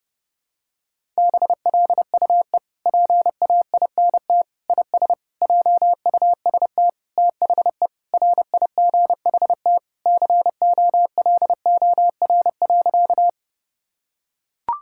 Morse code, heard as a keyed tone, about 30 wpm, 700 Hz. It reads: BLUE PAINT IS JUST THE RIGHT COLOR.
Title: Sentences from Top 1000 Words - Review Part 6 - 30wpm